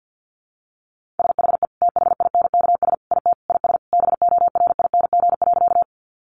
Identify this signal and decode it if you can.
Morse code, heard as a keyed tone, about 50 wpm, 700 Hz. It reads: SHE THINKS IT IS BORING.
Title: Easy Phrases 50wpm